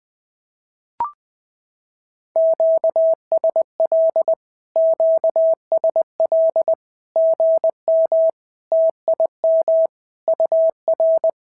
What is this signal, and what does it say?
Morse code, heard as a keyed tone, about 19 wpm, 650 Hz. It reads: QSL QSL GM TIM UR
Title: SOTA QSO Practice 20wpm